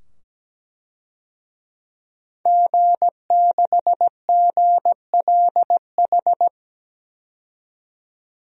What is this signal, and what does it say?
Morse code, heard as a keyed tone, about 17 wpm, 700 Hz. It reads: G6GLH
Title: DXCC-100 Call Signs - 5 Characters - 17wpm